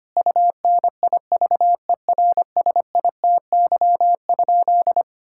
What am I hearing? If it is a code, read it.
Morse code, heard as a keyed tone, about 25 wpm, 700 Hz. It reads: UNIVERSITY?
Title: Sentences from Top 1000 Words - Review Part 6 - 25wpm